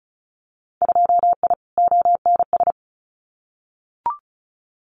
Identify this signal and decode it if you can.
Morse code, heard as a keyed tone, about 36 wpm, 700 Hz. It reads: 2I ONS
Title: Winter Field Day Exchanges Encore 35wpm